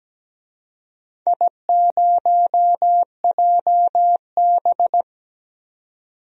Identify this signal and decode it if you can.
Morse code, heard as a keyed tone, about 17 wpm, 700 Hz. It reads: I0JB
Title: DXCC-100 Call Signs - 4 Characters - 17wpm